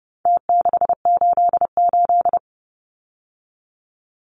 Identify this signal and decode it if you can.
Morse code, heard as a keyed tone, about 30 wpm, 700 Hz. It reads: T688
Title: Numbers - Contest Style - 4-Digit 30wpm